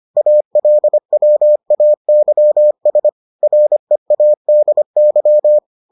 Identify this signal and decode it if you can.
Morse code, heard as a keyed tone, about 25 wpm, 600 Hz. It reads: ALWAYS READY